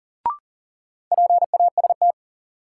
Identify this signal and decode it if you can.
Morse code, heard as a keyed tone, about 40 wpm, 700 Hz. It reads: PAST
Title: Sentences from Top 600 Words - New Word Drill - 40wpm